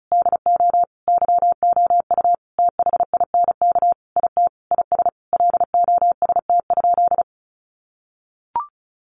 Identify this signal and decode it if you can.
Morse code, heard as a keyed tone, about 34 wpm, 700 Hz. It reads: DO YOU THINK IT IS LOST?
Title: Sentences from Top 1400 Words 35wpm